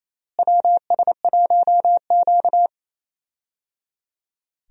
Morse code, heard as a keyed tone, about 28 wpm, 700 Hz. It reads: WS1Q
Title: USA Call Signs - 4 Characters - Repeated 3x - 28wpm